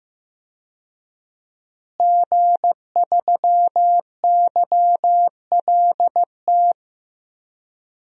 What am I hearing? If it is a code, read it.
Morse code, heard as a keyed tone, about 15 wpm, 700 Hz. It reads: G3YLT
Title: DXCC-100 Call Signs - 5 Characters - Repeated 2x - 15wpm